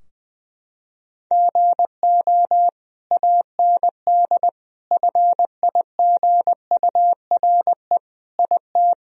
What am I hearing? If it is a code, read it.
Morse code, heard as a keyed tone, about 20 wpm, 700 Hz. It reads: GO AND FIGURE IT
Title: Sentences from Top 1000 Words - Review Part 2 - 20wpm